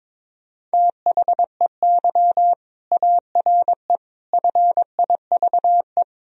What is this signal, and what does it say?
Morse code, heard as a keyed tone, about 22 wpm, 700 Hz. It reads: THEY ARE FIVE
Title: Sentences from Top 1000 Words - Review Part 2 - 22wpm